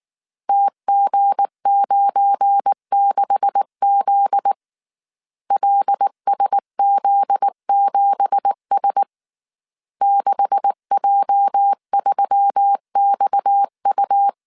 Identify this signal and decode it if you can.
Morse code, heard as a keyed tone, about 19 wpm, 800 Hz. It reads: TG96Z LSZ7S 6J3XU
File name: CW / DET / 1 9 1500C S A 18wpm 2459